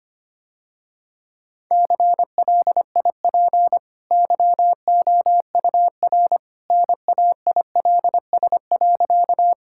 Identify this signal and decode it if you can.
Morse code, heard as a keyed tone, about 25 wpm, 700 Hz. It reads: CLIP YOUR NAILS.
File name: Easy Sentences with Hints - 25wpm